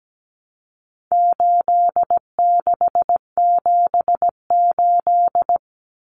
Morse code, heard as a keyed tone, about 17 wpm, 700 Hz. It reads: 8678
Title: Numbers - 4-Digit 17wpm